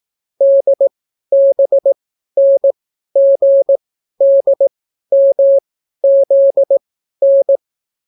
Morse code, heard as a keyed tone, about 18 wpm, spaced as for 12 wpm, 550 Hz. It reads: DBNGDMZN